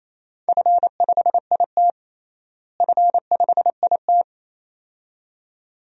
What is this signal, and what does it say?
Morse code, heard as a keyed tone, about 28 wpm, 700 Hz. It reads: F5IT F5IT
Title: DXCC-100 Call Signs - 4 Characters - Repeated 2x - 28wpm